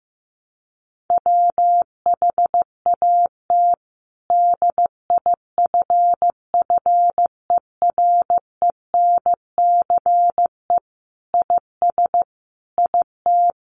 Morse code, heard as a keyed tone, about 15 wpm, 700 Hz. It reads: WHAT DIFFERENCE IS IT
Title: Sentences from Top 800 Words 15wpm